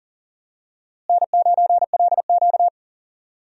Morse code, heard as a keyed tone, about 37 wpm, 700 Hz. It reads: N9LQ
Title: USA Call Signs - 4 Characters - Repeated 3x - 40wpm